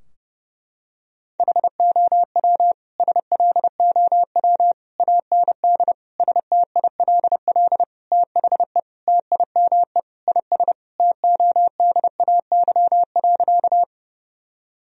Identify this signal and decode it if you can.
Morse code, heard as a keyed tone, about 32 wpm, 700 Hz. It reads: HOW SLOW AND STILL THE TIME IS TODAY.
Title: Sentences from Top 1000 Words - Review Part 2 - 30wpm